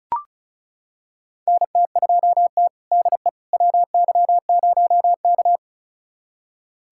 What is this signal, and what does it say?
Morse code, heard as a keyed tone, about 36 wpm, 700 Hz. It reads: NT2T DE WY0K
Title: USA Callsign DE Callsign - 4 Characters - 35wpm